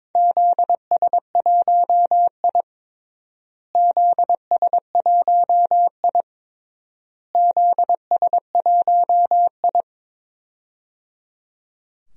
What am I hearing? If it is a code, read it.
Morse code, heard as a keyed tone, about 22 wpm, 700 Hz. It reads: ZS1I ZS1I ZS1I
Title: DXCC-100 Call Signs - 4 Characters - Repeated 3x - 22wpm